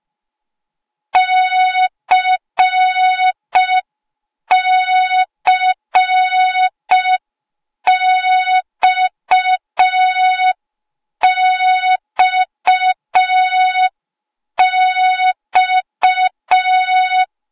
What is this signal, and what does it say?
Morse code, heard as a keyed tone, about 5 wpm, 3700 Hz. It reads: CCXXX
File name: Morze-HA2MN / hang / 08KXC HA2MN